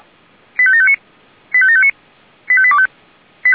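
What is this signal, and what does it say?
Five-tone selective call (ZVEI), heard as quick runs of five tones, tones of about 70 ms. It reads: 85759 85759 85825
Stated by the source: Five-tone ZVEI selective call received by SDR.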